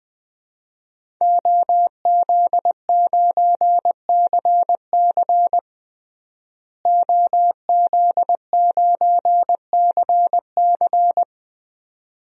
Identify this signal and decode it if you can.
Morse code, heard as a keyed tone, about 20 wpm, 700 Hz. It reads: OZ9CC OZ9CC
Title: DXCC-100 Call Signs - 5 Characters - Repeated 2x - 20wpm